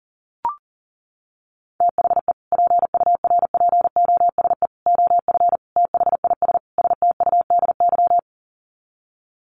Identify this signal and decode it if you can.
Morse code, heard as a keyed tone, about 40 wpm, 700 Hz. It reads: THE PURPOSE OF THIS STUDY
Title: Sets of 5 Words 40wpm